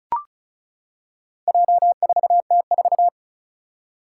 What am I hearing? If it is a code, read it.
Morse code, heard as a keyed tone, about 34 wpm, 700 Hz. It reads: J4T4
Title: Four Challenging Letter-Number Combinations 35wpm